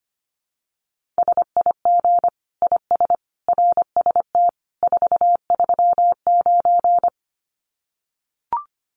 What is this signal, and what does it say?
Morse code, heard as a keyed tone, about 25 wpm, 700 Hz. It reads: SIG IS RST 439
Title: Common QSO Phrases 25wpm